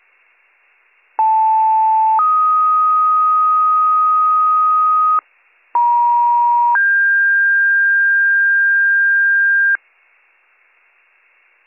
Two-tone page, heard as a one-second tone, then a three-second tone, about 880 then 1250 Hz, about 950 then 1600 Hz. It is heard from 1.2 to 9.8 s.